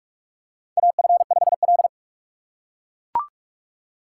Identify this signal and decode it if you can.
Morse code, heard as a keyed tone, about 41 wpm, 700 Hz. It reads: AF5L
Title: USA Call Signs - 4 Characters - 3x Character Spacing - 45wpm